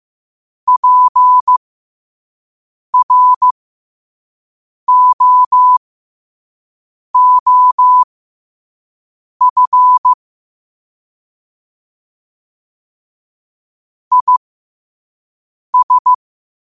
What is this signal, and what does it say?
Morse code, heard as a keyed tone, about 15 wpm, spaced as for 5 wpm, 1000 Hz. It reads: PROOF IS
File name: qotdfiles / 5wpm